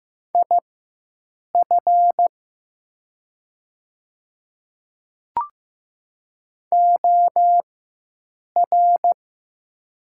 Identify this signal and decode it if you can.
Morse code, heard as a keyed tone, about 15 wpm, spaced as for 7 wpm, 700 Hz. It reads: IF OR
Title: 4x Character Spacing - Top 100 Words - 15wpm